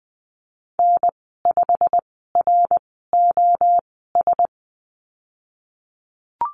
Morse code, heard as a keyed tone, about 20 wpm, spaced as for 15 wpm, 700 Hz. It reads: N5ROS
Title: 088 Lesson 15g - Number 5 - Call Signs 20wpm